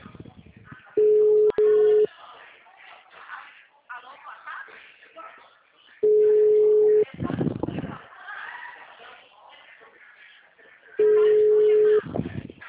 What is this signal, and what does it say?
Ringback tone (Europe, 425 Hz), with speech.